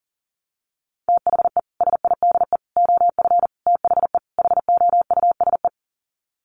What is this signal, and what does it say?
Morse code, heard as a keyed tone, about 40 wpm, 700 Hz. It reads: THE SIDE OF THE HOUSE
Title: Sets of 5 Words Encore2 40wpm